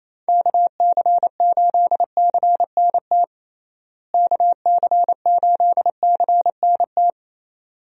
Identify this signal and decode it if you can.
Morse code, heard as a keyed tone, about 28 wpm, 700 Hz. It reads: KC8CNT KC8CNT
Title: USA Call Signs - 6 Characters - Repeated 2x - 28wpm